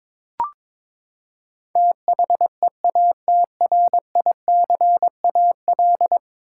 Morse code, heard as a keyed tone, about 22 wpm, 700 Hz. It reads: THEATRICAL
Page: morse-code-ninja-practice-22wpm